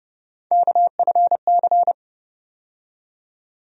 Morse code, heard as a keyed tone, about 30 wpm, 700 Hz. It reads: KFC
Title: USA Call Sign Postfixes - 3 Letters 30wpm